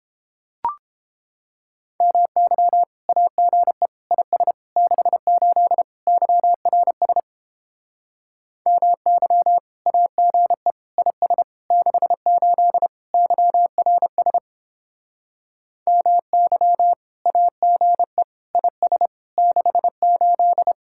Morse code, heard as a keyed tone, about 31 wpm, 700 Hz. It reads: MY AGE IS 68 YRS MY AGE IS 68 YRS MY AGE IS 68